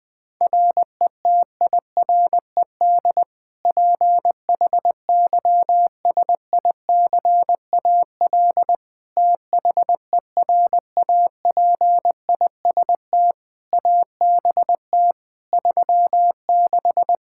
Morse code, heard as a keyed tone, about 20 wpm, 700 Hz. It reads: RETIRED PHYSICAL THERAPIST ABT 36
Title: Practice QSO - Exchange 3 - More Info - 20wpm